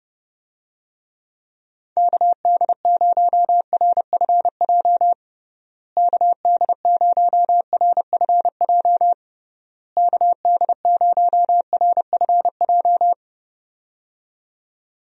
Morse code, heard as a keyed tone, about 30 wpm, 700 Hz. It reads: KD0RFJ KD0RFJ KD0RFJ
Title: USA Call Signs - 6 Characters - Repeated 3x - 30wpm